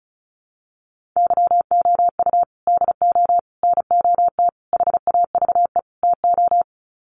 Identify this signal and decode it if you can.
Morse code, heard as a keyed tone, about 34 wpm, 700 Hz. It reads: YOU DO NOT HAVE TO
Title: Sets of 5 Words 35wpm